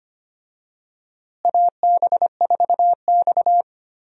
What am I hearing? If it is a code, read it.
Morse code, heard as a keyed tone, about 25 wpm, 700 Hz. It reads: AB4X
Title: USA Call Signs - 4 Characters - Repeated 3x - 25wpm